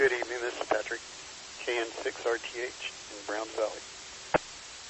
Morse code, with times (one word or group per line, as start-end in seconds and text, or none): none